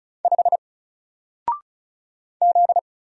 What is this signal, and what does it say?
Morse code, heard as a keyed tone, about 36 wpm, 700 Hz. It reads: 5 Z
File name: Single-Letter-Number-35wpm